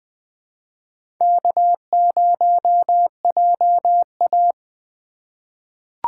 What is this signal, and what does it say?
Morse code, heard as a keyed tone, about 20 wpm, 700 Hz. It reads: K0JA
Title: USA Call Signs - 4x Character Spacing - 20wpm